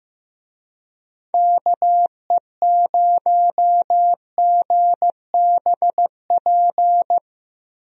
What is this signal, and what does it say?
Morse code, heard as a keyed tone, about 15 wpm, 700 Hz. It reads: KE0GBP
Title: USA Call Signs - 6 Characters - Repeated 3x - 15wpm